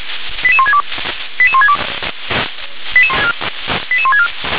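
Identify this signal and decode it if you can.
Five-tone selective call (ZVEI), heard as quick runs of five tones, tones of about 70 ms. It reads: 88162 88162 88164
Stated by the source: Five-tone ZVEI selective call 88158 received by SDR.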